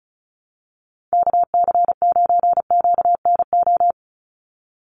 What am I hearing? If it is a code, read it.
Morse code, heard as a keyed tone, about 34 wpm, 700 Hz. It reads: KC9QNO
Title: USA Call Signs 35wpm